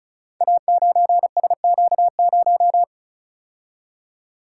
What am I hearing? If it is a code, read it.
Morse code, heard as a keyed tone, about 36 wpm, 690 Hz. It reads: A9SQ0
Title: Code Groups - 5 Letter:Numbers - 35wpm